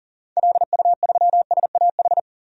Morse code, heard as a keyed tone, about 40 wpm, 700 Hz. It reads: LU3SAH
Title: DXCC-100 Call Signs - 6 Characters - 40wpm